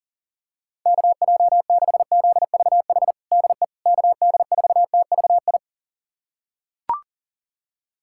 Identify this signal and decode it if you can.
Morse code, heard as a keyed tone, about 40 wpm, 700 Hz. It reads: KJ6ZVH DE KD4TVI
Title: USA Callsign DE Callsign 40wpm